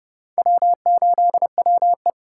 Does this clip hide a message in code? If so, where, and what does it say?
Yes: Morse code from 0.4 to 2.1 s, heard as a keyed tone, about 34 wpm, 700 Hz. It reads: W8WE